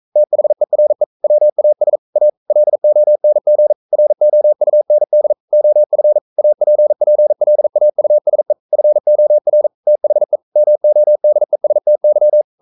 Morse code, heard as a keyed tone, about 42 wpm, 600 Hz. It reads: THERE WAS A LONG ROUND OF APPLAUSE FOR THE MODESTY
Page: Romanen Never af Ken Follet på engelsk.